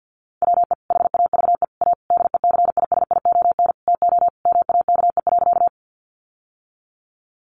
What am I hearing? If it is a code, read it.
Morse code, heard as a keyed tone, about 48 wpm, 700 Hz. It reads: WE HAVE A DECISION TO MAKE.
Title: Sentences from Top 1000 Words 50wpm